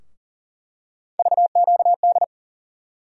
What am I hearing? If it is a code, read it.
Morse code, heard as a keyed tone, about 40 wpm, 700 Hz. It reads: VQD
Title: USA Call Sign Postfixes - 3 Letters 40wpm